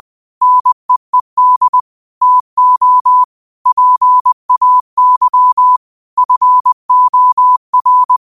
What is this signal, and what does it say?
Morse code, heard as a keyed tone, about 20 wpm, 1000 Hz. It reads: NEED TO PAY FOR
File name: 20wpm